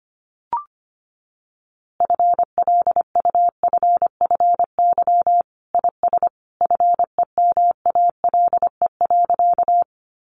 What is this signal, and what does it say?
Morse code, heard as a keyed tone, about 25 wpm, 700 Hz. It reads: FLUFFY IS FEMALE.
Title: Easy Phrases 25wpm